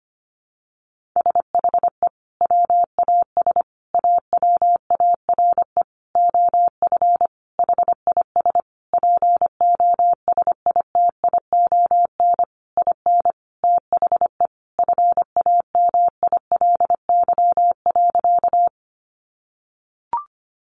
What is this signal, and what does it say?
Morse code, heard as a keyed tone, about 25 wpm, 700 Hz. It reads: SHE WAS AWARE OF HIS POSITION IN THE FAMILY.
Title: Sentences from Top 1200 Words 25wpm